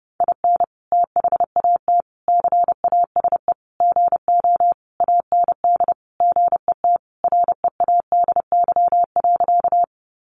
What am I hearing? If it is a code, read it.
Morse code, heard as a keyed tone, about 30 wpm, 700 Hz. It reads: IN THAT CASE GO AND GET READY.
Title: Sentences from Top 700 Words 30wpm